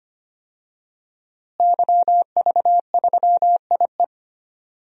Morse code, heard as a keyed tone, about 25 wpm, 700 Hz. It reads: YV3IE
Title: DXCC-100 Call Signs - 5 Characters - Repeated 2x - 25wpm